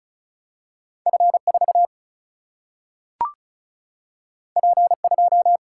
Morse code, heard as a keyed tone, about 36 wpm, 700 Hz. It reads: F 4 P 2
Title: Single-Letter Single-Number 35wpm